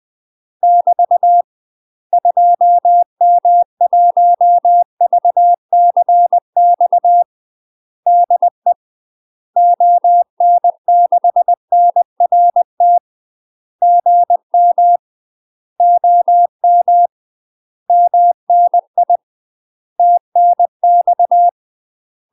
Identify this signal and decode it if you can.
Morse code, heard as a keyed tone, about 20 wpm, 700 Hz. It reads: <BT> 2M1VCX DE ON6NRT GM OM MNI TNX
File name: UK MASTER QSOs TRAINING / 20WPM / UK MASTER QSOs 0055